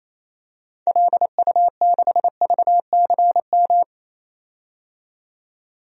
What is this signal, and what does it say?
Morse code, heard as a keyed tone, about 28 wpm, 700 Hz. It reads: LU6VCM
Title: DXCC-100 Call Signs - 6 Characters - 28wpm